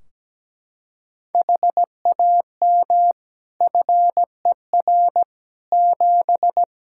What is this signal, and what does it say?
Morse code, heard as a keyed tone, about 17 wpm, 700 Hz. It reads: HAM FER 7
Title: Practice QSO - Exchange 3 - More Info - 17wpm